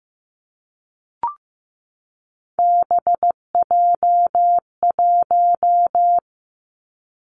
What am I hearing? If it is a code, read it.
Morse code, heard as a keyed tone, about 15 wpm, 700 Hz. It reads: BJ1
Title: Three Challenging Letter-Number Combinations 15wpm